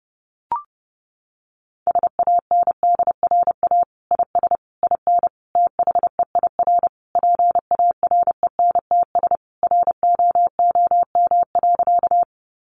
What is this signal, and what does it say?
Morse code, heard as a keyed tone, about 30 wpm, 700 Hz. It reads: SANDRA IS IN THEIR PARENTS ROOM.
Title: Easy Phrases 30wpm